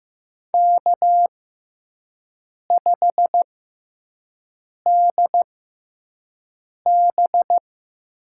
Morse code, heard as a keyed tone, about 15 wpm, spaced as for 5 wpm, 700 Hz. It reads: K5DB
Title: USA Call Signs - 6x Character Spacing - 15wpm